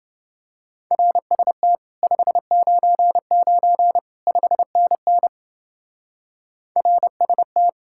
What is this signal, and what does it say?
Morse code, heard as a keyed tone, about 28 wpm, 700 Hz. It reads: RST 599 5NN RST